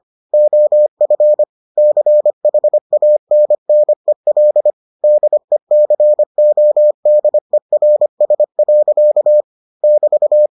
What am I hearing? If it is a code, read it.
Morse code, heard as a keyed tone, about 25 wpm, 600 Hz. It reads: OF CHANNEL DECODERS. <BT>